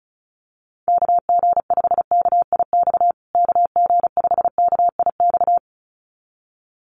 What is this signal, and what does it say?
Morse code, heard as a keyed tone, about 34 wpm, 700 Hz. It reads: KG5KIX KG5KIX
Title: USA Call Signs Repeated 2x 35wpm